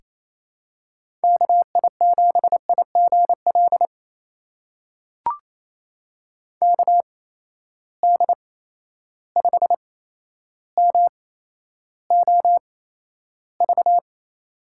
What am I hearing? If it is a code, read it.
Morse code, heard as a keyed tone, about 28 wpm, spaced as for 12 wpm, 700 Hz. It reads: KI7IGL KD5MOV
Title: USA Call Signs - 6 Characters - 8x Character Spacing - 28wpm